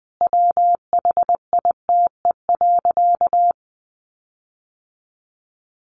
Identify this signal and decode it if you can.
Morse code, heard as a keyed tone, about 20 wpm, 700 Hz. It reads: WHITE.
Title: Sentences from Top 1500 Words 20wpm